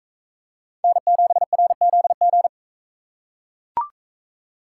Morse code, heard as a keyed tone, about 40 wpm, 700 Hz. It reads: N7RZG